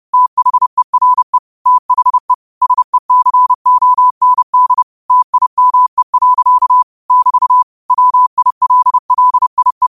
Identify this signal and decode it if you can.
Morse code, heard as a keyed tone, about 30 wpm, 1000 Hz. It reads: THERE THE SECOND TIME. <BT> WILLIE
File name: qotdfiles / 30wpm